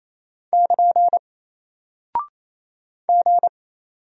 Morse code, heard as a keyed tone, about 28 wpm, 700 Hz. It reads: <KN> G
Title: Single Character (Letter:Number:Punctuation:Prosign) Alt 28wpm